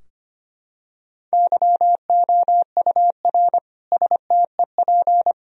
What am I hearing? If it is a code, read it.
Morse code, heard as a keyed tone, about 25 wpm, 700 Hz. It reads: YOUR STEP
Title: Sentences from Top 1000 Words - Review Part 2 - 25wpm